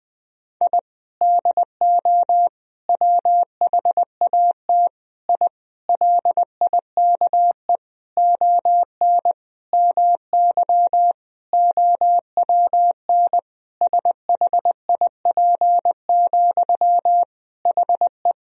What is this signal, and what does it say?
Morse code, heard as a keyed tone, about 20 wpm, 700 Hz. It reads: I DO WHAT I LIKE ON MY OWN SHIP, HE
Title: Sentences from Top 1000 Words - Review Part 2 - 20wpm